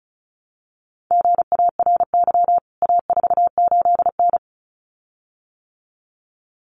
Morse code, heard as a keyed tone, about 34 wpm, 700 Hz. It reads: GARY A48N
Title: CWT Exchange 35wpm